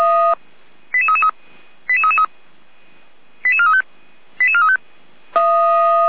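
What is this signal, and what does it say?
Five-tone selective call (ZVEI), heard as quick runs of five tones, tones of about 70 ms. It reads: 88333 88333 88435 88435
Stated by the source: ZVEI five-tone call received by SDR, from the Feuerwehr Orlamünde audio server.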